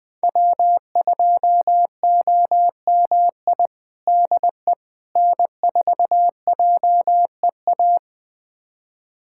Morse code, heard as a keyed tone, about 21 wpm, 700 Hz. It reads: W2OMI DE N4JEA
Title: USA Callsign DE Callsign - 5 Characters - 20wpm